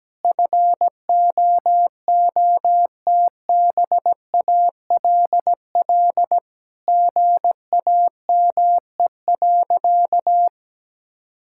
Morse code, heard as a keyed tone, about 17 wpm, 700 Hz. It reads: FOOTBALL GAME.
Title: Sentences from Top 1600 Words 17wpm